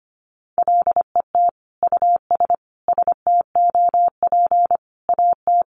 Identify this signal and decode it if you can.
Morse code, heard as a keyed tone, about 25 wpm, 700 Hz. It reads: LET US STOP AT
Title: Sentences from Top 400 Words 25wpm